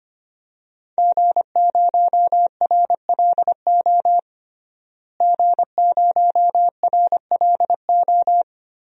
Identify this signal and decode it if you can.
Morse code, heard as a keyed tone, about 25 wpm, 700 Hz. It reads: G0RLO G0RLO
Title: DXCC-100 Call Signs - 5 Characters - Repeated 3x - 25wpm